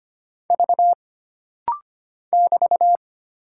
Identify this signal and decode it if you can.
Morse code, heard as a keyed tone, about 25 wpm, 700 Hz. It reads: V <BT>
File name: Single Character (Letter:Number:Punctuation:Prosign) - Rapid-Fire - 25wpm